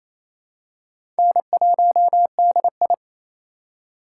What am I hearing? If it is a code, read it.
Morse code, heard as a keyed tone, about 27 wpm, 700 Hz. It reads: N1DI
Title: USA Call Signs - 4x Character Spacing - 28wpm